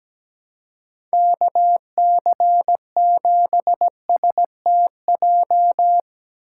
Morse code, heard as a keyed tone, about 17 wpm, 700 Hz. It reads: KC7STJ
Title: USA Call Signs - 6 Characters - 4x Character Spacing - 17wpm